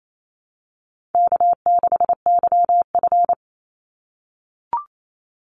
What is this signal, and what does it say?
Morse code, heard as a keyed tone, about 28 wpm, 700 Hz. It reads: K6YF
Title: Easy Call Signs - 28wpm